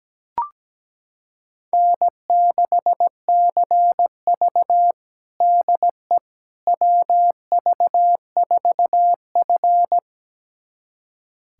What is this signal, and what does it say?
Morse code, heard as a keyed tone, about 17 wpm, 700 Hz. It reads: N6CV DE WV4F
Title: USA Callsign DE Callsign - 4 Characters - 17wpm